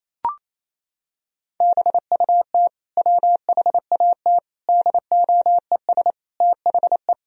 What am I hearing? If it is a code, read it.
Morse code, heard as a keyed tone, about 28 wpm, 700 Hz. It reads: BUT WHAT DOES THE
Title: Sentences from Top 1000 Words - Review Part 2 - 28wpm